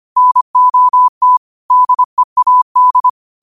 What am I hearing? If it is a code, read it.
Morse code, heard as a keyed tone, about 25 wpm, 1000 Hz. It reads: NOT DEAD
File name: qotdfiles / 25wpm